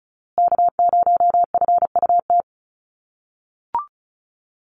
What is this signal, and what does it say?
Morse code, heard as a keyed tone, about 34 wpm, 700 Hz. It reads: K0FUT
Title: USA Call Signs Repeated 3x 35wpm